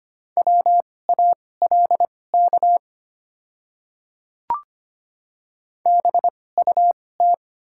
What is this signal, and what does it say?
Morse code, heard as a keyed tone, about 25 wpm, spaced as for 18 wpm, 700 Hz. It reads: WALK BUT
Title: 2x Character Spacing - Top 500 Words - 25wpm